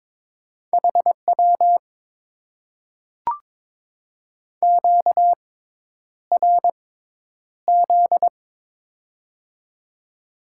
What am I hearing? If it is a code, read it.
Morse code, heard as a keyed tone, about 22 wpm, spaced as for 10 wpm, 700 Hz. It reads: HW QRZ
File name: Top 90 QSO Elements - 6x Character Spacing - 22wpm